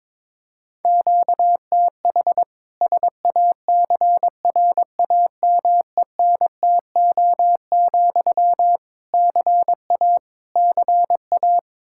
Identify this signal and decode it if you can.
Morse code, heard as a keyed tone, about 22 wpm, 700 Hz. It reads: QTH SACRAMENTO, CA CA
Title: K1USN SST QSO Practice 22wpm